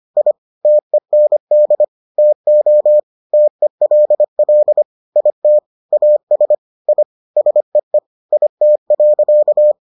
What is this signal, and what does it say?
Morse code, heard as a keyed tone, about 25 wpm, 600 Hz. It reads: I TEND TO TELL IT AS I SEE IT.